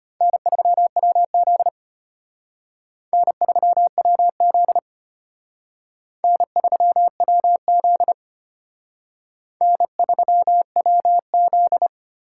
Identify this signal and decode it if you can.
Morse code, heard as a keyed tone, about 33 wpm, 700 Hz. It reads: N3WZ N3WZ N3WZ N3WZ